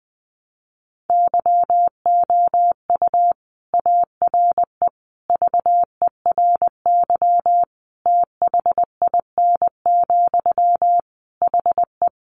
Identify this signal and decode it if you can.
Morse code, heard as a keyed tone, about 19 wpm, 700 Hz. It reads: YOU ARE VERY THIN, HE
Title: Sentences from Top 1200 Words 20wpm